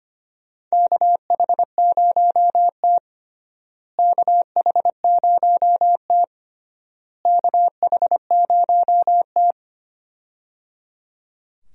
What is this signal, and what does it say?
Morse code, heard as a keyed tone, about 25 wpm, 700 Hz. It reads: KH0T KH0T KH0T
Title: USA Call Signs - 4 Characters - Repeated 3x - 25wpm